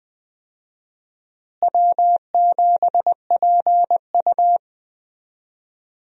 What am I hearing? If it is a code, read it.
Morse code, heard as a keyed tone, about 20 wpm, 700 Hz. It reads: W7PU
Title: USA Call Signs - 4 Characters - Repeated 2x - 20wpm